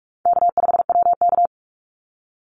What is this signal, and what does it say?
Morse code, heard as a keyed tone, about 44 wpm, 700 Hz. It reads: K5WK
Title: Easy Call Signs - 45wpm